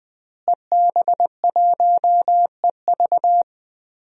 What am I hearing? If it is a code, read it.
Morse code, heard as a keyed tone, about 20 wpm, 700 Hz. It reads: EB1EV
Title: DXCC-100 Call Signs - 5 Characters - Repeated 2x - 20wpm